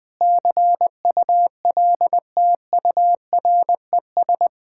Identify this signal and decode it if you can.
Morse code, heard as a keyed tone, about 20 wpm, 690 Hz. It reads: CULTURES
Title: Sentences with 4-Word Phrases - Spoken Beginning - 20wpm